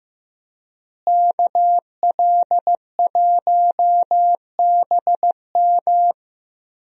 Morse code, heard as a keyed tone, about 15 wpm, 700 Hz. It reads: KL1BM
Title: USA Call Signs - 5 Characters - Repeated 3x - 15wpm